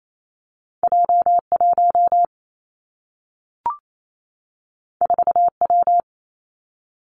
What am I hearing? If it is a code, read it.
Morse code, heard as a keyed tone, about 28 wpm, 700 Hz. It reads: J 1 4 W
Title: Two Challenging Letter-Number Combinations 28wpm